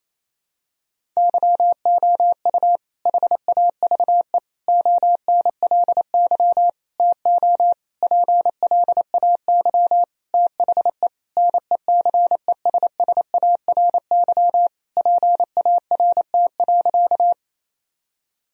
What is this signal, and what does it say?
Morse code, heard as a keyed tone, about 28 wpm, 700 Hz. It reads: YOU HAVE ONLY TO PLAY THE NECESSARY PART.
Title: Sentences from Top 1000 Words - Review Part 1 - 28wpm